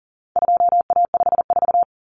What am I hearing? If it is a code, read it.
Morse code, heard as a keyed tone, about 40 wpm, 700 Hz. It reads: 2A54
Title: Common Cut Numbers - 4-Digit 40wpm